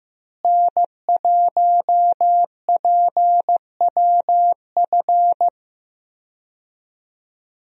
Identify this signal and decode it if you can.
Morse code, heard as a keyed tone, about 15 wpm, 700 Hz. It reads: N1PWF
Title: USA Call Signs - 5 Characters - Repeated 3x - 15wpm